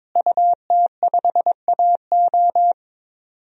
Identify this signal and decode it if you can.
Morse code, heard as a keyed tone, about 21 wpm, 700 Hz. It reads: UT5AO
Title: DXCC-100 Call Signs - 5 Characters - Repeated 3x - 22wpm